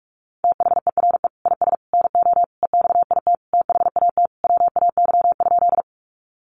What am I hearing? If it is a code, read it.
Morse code, heard as a keyed tone, about 44 wpm, 700 Hz. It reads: THERE IS NO EXIT THAT WAY?
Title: Sentences from Top 1100 Words 45wpm